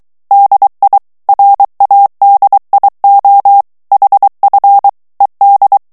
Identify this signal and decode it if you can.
Morse code, heard as a keyed tone, about 23 wpm, 800 Hz. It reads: DI RADIO HF ED
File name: morse mp3 / ITALIEN texte clair 1 vitesse 22 mots minute